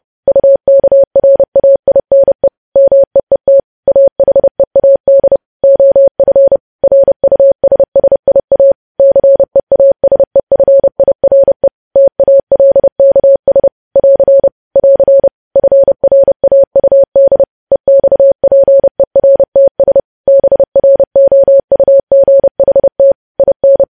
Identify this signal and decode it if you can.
Morse code, heard as a keyed tone, about 30 wpm, 550 Hz. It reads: UKRAINE MEET AHEAD OF RUSSIA CEASEFIRE TALKS <AR> <AR> FRAUD EXPERTS BROUGHT IN
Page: news headlines in morse code at 30 wpm